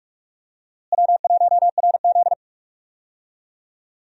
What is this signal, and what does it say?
Morse code, heard as a keyed tone, about 46 wpm, 700 Hz. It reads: W1RZ
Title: USA Call Signs - 4 Characters - Repeated 2x - 45wpm